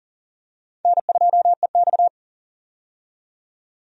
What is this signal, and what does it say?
Morse code, heard as a keyed tone, about 37 wpm, 700 Hz. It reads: N2EX
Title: USA Call Signs - 4 Characters - 40wpm